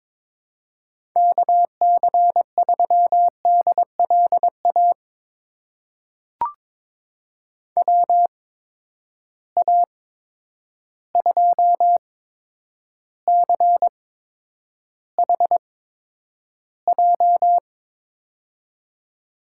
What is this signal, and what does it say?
Morse code, heard as a keyed tone, about 22 wpm, spaced as for 9 wpm, 700 Hz. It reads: KC3DLA WA2CHJ
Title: USA Call Signs - 6 Characters - 8x Character Spacing - 22wpm